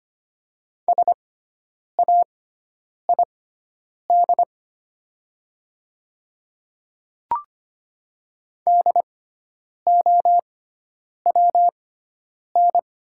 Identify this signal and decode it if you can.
Morse code, heard as a keyed tone, about 25 wpm, spaced as for 9 wpm, 700 Hz. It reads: SAID DOWN
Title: 6x Character Spacing - Top 100 Words - 25wpm